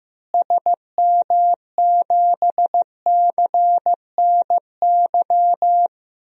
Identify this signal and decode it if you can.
Morse code, heard as a keyed tone, about 15 wpm, 700 Hz. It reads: SM7CNY
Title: DXCC-100 Call Signs - 6 Characters - Repeated 3x - 15wpm